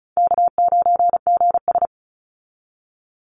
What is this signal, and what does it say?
Morse code, heard as a keyed tone, about 34 wpm, 700 Hz. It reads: K9GS
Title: Easy Call Signs Repeated 3x - 35wpm